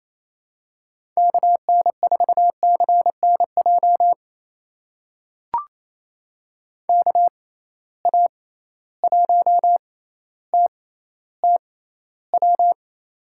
Morse code, heard as a keyed tone, about 28 wpm, spaced as for 14 wpm, 700 Hz. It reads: KN4CNJ KA1TTW
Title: USA Call Signs - 6 Characters - 6x Character Spacing - 28wpm